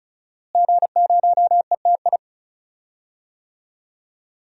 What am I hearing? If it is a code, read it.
Morse code, heard as a keyed tone, about 34 wpm, 700 Hz. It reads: G0ETI
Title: DXCC-100 Call Signs - 5 Characters - Repeated 3x - 35wpm